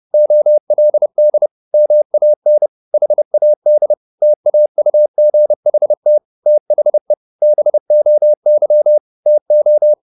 Morse code, heard as a keyed tone, about 30 wpm, 600 Hz. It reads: OLD MAN HAD TAUGHT THE BOY TO
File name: the old man and the sea / the old man and the sea 1 30wpm